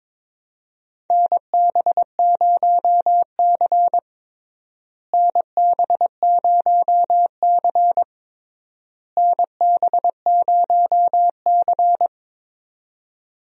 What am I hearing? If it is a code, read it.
Morse code, heard as a keyed tone, about 22 wpm, 700 Hz. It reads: NB0C NB0C NB0C
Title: USA Call Signs - 4 Characters - Repeated 3x - 22wpm